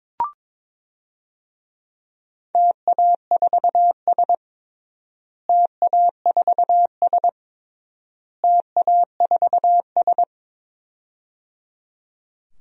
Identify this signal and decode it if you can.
Morse code, heard as a keyed tone, about 22 wpm, 700 Hz. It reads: TA4S TA4S TA4S
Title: DXCC-100 Call Signs - 4 Characters - Repeated 3x - 22wpm